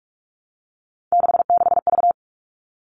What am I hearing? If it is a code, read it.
Morse code, heard as a keyed tone, about 44 wpm, 700 Hz. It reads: 66V
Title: All Cut Numbers - 3-Digit 45wpm